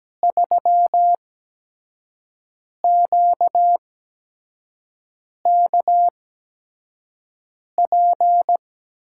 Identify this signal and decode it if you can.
Morse code, heard as a keyed tone, about 17 wpm, spaced as for 5 wpm, 700 Hz. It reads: 3QKP
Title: USA Call Signs - 8x Character Spacing - 17wpm